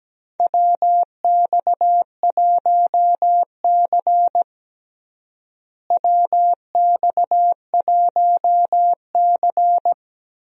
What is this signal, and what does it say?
Morse code, heard as a keyed tone, about 17 wpm, 700 Hz. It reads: WX1C WX1C
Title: USA Call Signs - 4 Characters - Repeated 2x - 17wpm